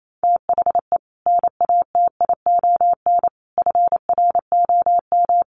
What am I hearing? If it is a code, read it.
Morse code, heard as a keyed tone, about 28 wpm, 700 Hz. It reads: THE NATION FROM
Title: Sets of 5 Words 28wpm